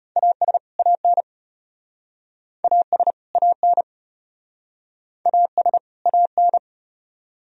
Morse code, heard as a keyed tone, about 35 wpm, 700 Hz. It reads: AS AN AS AN AS AN